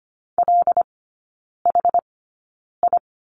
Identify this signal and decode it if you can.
Morse code, heard as a keyed tone, about 25 wpm, spaced as for 9 wpm, 700 Hz. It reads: LHI
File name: Lesson 17e - Letter u - Three Characters 25wpm